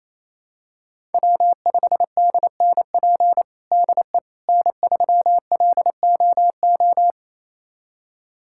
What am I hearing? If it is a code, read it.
Morse code, heard as a keyed tone, about 28 wpm, 700 Hz. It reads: W5DNP DE N3LOO
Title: USA Callsign DE Callsign - 5 Characters - 28wpm